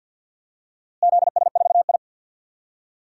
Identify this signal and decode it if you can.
Morse code, heard as a keyed tone, about 47 wpm, 700 Hz. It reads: ZS4I
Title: DXCC-100 Call Signs - 4 Characters - Repeated 2x - 50wpm